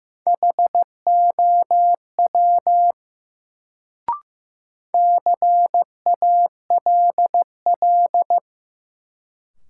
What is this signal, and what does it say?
Morse code, heard as a keyed tone, about 15 wpm, 700 Hz. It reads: HOW CALL
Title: Sentences from Top 100 Words - New Word Drill - 15wpm